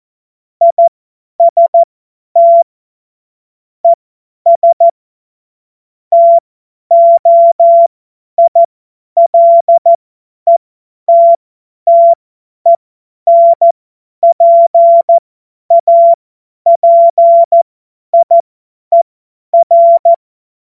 Morse code, heard as a keyed tone, about 14 wpm, spaced as for 10 wpm, 680 Hz. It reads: IST ES TOILETTENPAPIER
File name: Morsekurs / Witze / Witz 07. September 2021, 50 F70